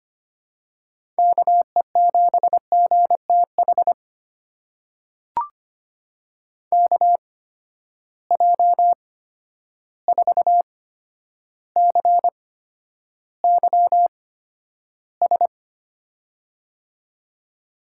Morse code, heard as a keyed tone, about 25 wpm, spaced as for 11 wpm, 700 Hz. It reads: KE7GTH KJ4CYS
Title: USA Call Signs - 6 Characters - 8x Character Spacing - 25wpm